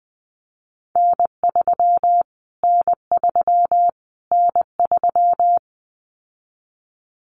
Morse code, heard as a keyed tone, about 20 wpm, 700 Hz. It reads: N3 N3 N3
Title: Callsign Prefixes for Top 25 DXCC - Repeated 3x - 20wpm